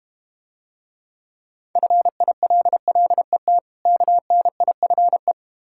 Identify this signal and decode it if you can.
Morse code, heard as a keyed tone, about 31 wpm, 700 Hz. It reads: FILLET KNIFE